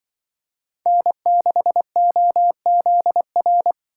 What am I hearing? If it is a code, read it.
Morse code, heard as a keyed tone, about 23 wpm, 700 Hz. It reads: N6OZR
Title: Speed Racing - USA 1x3 Call Signs 20wpm